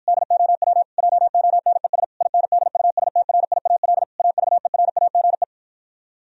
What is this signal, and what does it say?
Morse code, heard as a keyed tone, about 52 wpm, 700 Hz. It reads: DOW JONES INDUSTRIAL AVERAGE